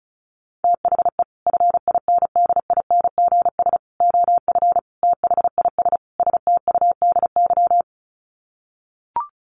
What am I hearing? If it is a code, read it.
Morse code, heard as a keyed tone, about 34 wpm, 700 Hz. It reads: THE FINDINGS OF THIS STUDY
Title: Sets of 5 Words Encore2 35wpm